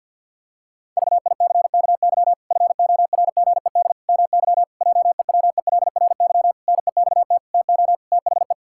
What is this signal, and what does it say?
Morse code, heard as a keyed tone, about 46 wpm, 700 Hz. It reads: VICKY FORGED MY JEWELRY NEXT TO THE
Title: Pangram Challenge v2 - 50wpm